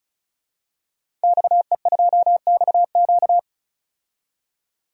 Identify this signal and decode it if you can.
Morse code, heard as a keyed tone, about 34 wpm, 700 Hz. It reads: XE2XQ
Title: DXCC-100 Call Signs - 5 Characters - 35wpm